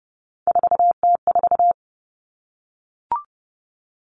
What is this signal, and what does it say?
Morse code, heard as a keyed tone, about 30 wpm, 700 Hz. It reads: 4T4
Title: Three Challenging Letter-Number Combinations 30wpm